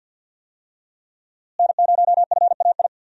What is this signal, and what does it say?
Morse code, heard as a keyed tone, about 53 wpm, 700 Hz. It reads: N0FAI
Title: USA Call Signs - 5 Characters - Repeated 3x - 50wpm